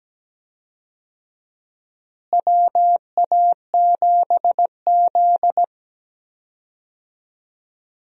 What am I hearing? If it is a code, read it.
Morse code, heard as a keyed tone, about 17 wpm, 700 Hz. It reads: WA7Z
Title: USA Call Signs - 4 Characters - Repeated 2x - 17wpm